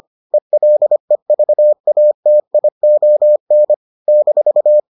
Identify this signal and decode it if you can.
Morse code, heard as a keyed tone, about 25 wpm, 600 Hz. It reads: ELEVATION -